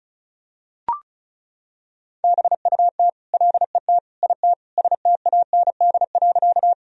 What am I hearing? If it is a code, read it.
Morse code, heard as a keyed tone, about 36 wpm, 700 Hz. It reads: BUT LET IT STAND.
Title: Sentences from Top 1000 Words - Review Part 1 - 35wpm